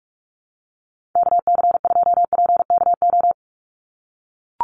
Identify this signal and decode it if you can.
Morse code, heard as a keyed tone, about 44 wpm, 700 Hz. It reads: KC2PKO
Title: USA Call Signs 45wpm